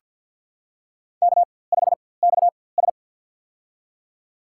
Morse code, heard as a keyed tone, about 50 wpm, spaced as for 23 wpm, 700 Hz. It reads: K5XS
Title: USA Call Signs - 4x Character Spacing - 50wpm